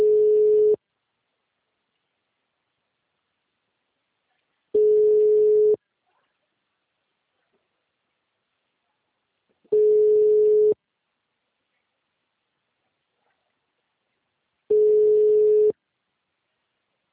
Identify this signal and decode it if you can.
Ringback tone (Europe, 425 Hz).
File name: out- -2011- - - .83942